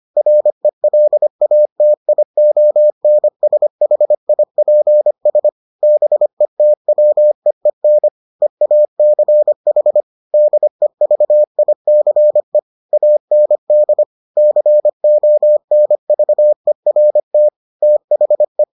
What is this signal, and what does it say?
Morse code, heard as a keyed tone, about 25 wpm, 600 Hz. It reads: RELATIONSHIPS BETWEEN EACH DEVICE AND CONVERT THE